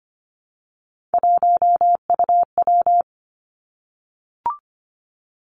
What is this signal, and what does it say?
Morse code, heard as a keyed tone, about 25 wpm, 700 Hz. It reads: 1UW
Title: Three Challenging Letter-Number Combinations 25wpm